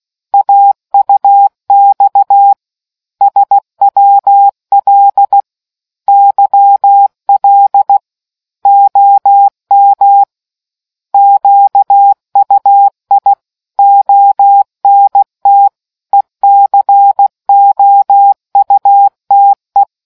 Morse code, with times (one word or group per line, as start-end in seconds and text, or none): 0.3-2.5 AUX
3.2-5.4 SWL
6.1-8.0 YL
8.7-10.2 OM
11.1-13.3 QUI
13.8-15.7 ONT
16.1-19.8 ECOUTE